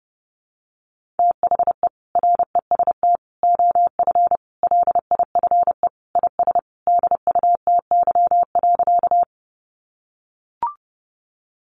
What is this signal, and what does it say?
Morse code, heard as a keyed tone, about 30 wpm, 700 Hz. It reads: THE REST OF LIFE IS DUTY.
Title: Sentences from Top 1500 Words 30wpm